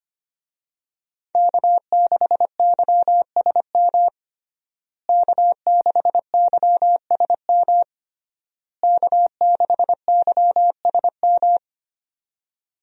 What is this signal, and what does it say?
Morse code, heard as a keyed tone, about 25 wpm, 700 Hz. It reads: K6YSM K6YSM K6YSM
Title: USA Call Signs - 5 Characters - Repeated 3x - 25wpm